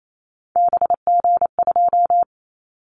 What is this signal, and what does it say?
Morse code, heard as a keyed tone, about 28 wpm, 700 Hz. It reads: BG2
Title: Callsign Prefixes for Top 100 DXCC - 28wpm